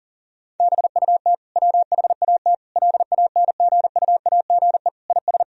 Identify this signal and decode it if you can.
Morse code, heard as a keyed tone, about 40 wpm, 700 Hz. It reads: BUT WHAT LANGUAGE IS
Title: Sentences from Top 1000 Words - Review Part 3 - 40wpm